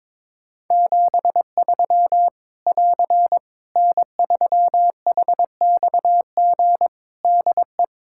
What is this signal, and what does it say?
Morse code, heard as a keyed tone, about 22 wpm, 700 Hz. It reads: 73 <AR> N3HXG DE
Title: K1USN SST QSO Practice 22wpm